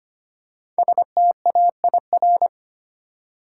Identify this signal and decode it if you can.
Morse code, heard as a keyed tone, about 25 wpm, 700 Hz. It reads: STAIR
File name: 5 Letter Words - First 3 Letters Given - 25wpm